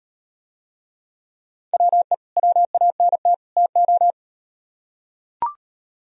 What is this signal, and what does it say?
Morse code, heard as a keyed tone, about 38 wpm, 700 Hz. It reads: WE WANT TO